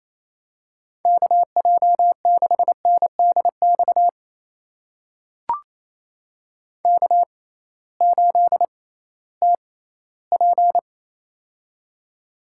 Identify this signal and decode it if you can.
Morse code, heard as a keyed tone, about 28 wpm, spaced as for 16 wpm, 700 Hz. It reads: KJ6NDX K8TP
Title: USA Call Signs - 6x Character Spacing - 28wpm